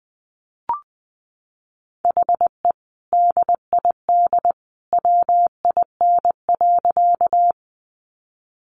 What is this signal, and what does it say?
Morse code, heard as a keyed tone, about 20 wpm, 700 Hz. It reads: HE DID WIN.
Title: Sentences from Top 1500 Words 20wpm